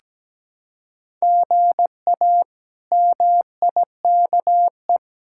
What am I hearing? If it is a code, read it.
Morse code, heard as a keyed tone, about 17 wpm, 700 Hz. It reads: GA MIKE
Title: K1USN SST QSO Practice - 17wpm